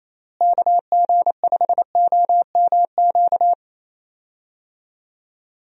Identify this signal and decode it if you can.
Morse code, heard as a keyed tone, about 27 wpm, 700 Hz. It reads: KG5OMQ
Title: USA Call Signs - 6 Characters - Repeated 2x - 28wpm